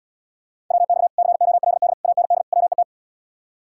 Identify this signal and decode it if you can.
Morse code, heard as a keyed tone, about 77 wpm, 700 Hz. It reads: WX COLD AND WET